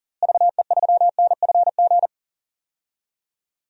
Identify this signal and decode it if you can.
Morse code, heard as a keyed tone, about 40 wpm, 700 Hz. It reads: VE3NFG
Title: DXCC-100 Call Signs - 6 Characters - Repeated 3x - 40wpm